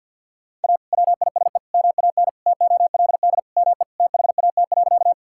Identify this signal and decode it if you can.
Morse code, heard as a keyed tone, about 50 wpm, 700 Hz. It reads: A WISE MAN TOLD ME THAT.
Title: Sentences from Top 1600 Words 50wpm